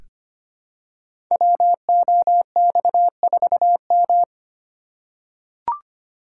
Morse code, heard as a keyed tone, about 25 wpm, 700 Hz. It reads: WOX4M
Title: Code Groups - 5 Letter:Numbers - 25wpm